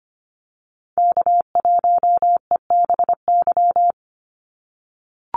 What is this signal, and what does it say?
Morse code, heard as a keyed tone, about 25 wpm, 700 Hz. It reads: K1EBY
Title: Call Signs - 25wpm